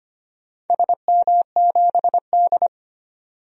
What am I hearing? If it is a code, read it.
Morse code, heard as a keyed tone, about 25 wpm, 700 Hz. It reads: SM7D
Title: DXCC-100 Call Signs - 4 Characters - 25wpm